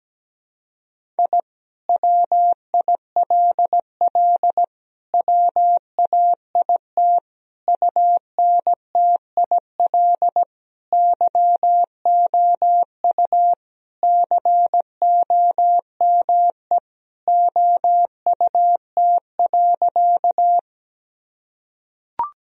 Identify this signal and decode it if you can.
Morse code, heard as a keyed tone, about 17 wpm, 700 Hz. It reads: I WILL WAIT UNTIL YOU COME OUT.
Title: Sentences from Top 1000 Words - Review Part 1 - 17wpm